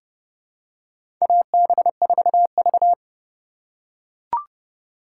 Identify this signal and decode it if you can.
Morse code, heard as a keyed tone, about 30 wpm, 700 Hz. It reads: AB4V
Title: USA Call Signs - 4 Characters - 4x Character Spacing - 30wpm